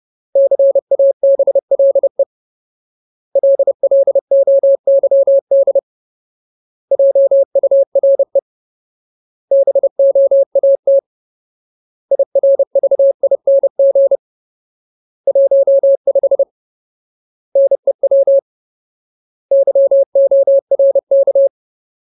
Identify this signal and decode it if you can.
Morse code, heard as a keyed tone, about 30 wpm, 550 Hz. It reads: CABLE LLOYD JURE BOAT IRVING 15 NEW YORK
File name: Ultimate Word List / 30WPM Ultimate